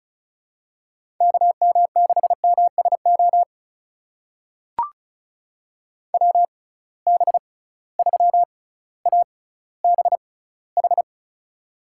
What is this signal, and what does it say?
Morse code, heard as a keyed tone, about 36 wpm, spaced as for 18 wpm, 700 Hz. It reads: KM6MSO WB3ABH
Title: USA Call Signs - 6 Characters - 6x Character Spacing - 35wpm